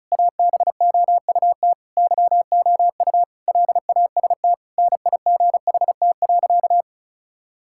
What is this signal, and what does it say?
Morse code, heard as a keyed tone, about 36 wpm, 700 Hz. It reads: ABOUT YOU LAST NIGHT.
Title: Sentences from Top 1600 Words 35wpm